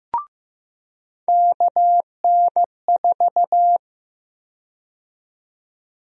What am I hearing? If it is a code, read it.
Morse code, heard as a keyed tone, about 15 wpm, 700 Hz. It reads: KN4
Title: USA Call Sign Prefixes - 15wpm